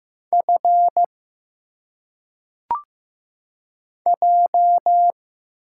Morse code, heard as a keyed tone, about 15 wpm, 700 Hz. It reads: F J
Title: Single Letters - Phonetic Pronunciation 15wpm